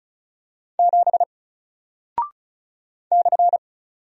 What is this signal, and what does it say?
Morse code, heard as a keyed tone, about 36 wpm, 700 Hz. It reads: 7 /
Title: Single Character (Letter:Number:Punctuation:Prosign) 35wpm